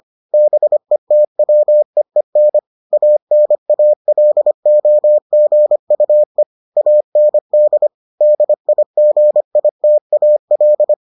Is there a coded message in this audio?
Yes: Morse code from 0.3 to 10.9 s, heard as a keyed tone, about 25 wpm, 600 Hz. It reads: BETWEEN ANALOGUE AND DIGITAL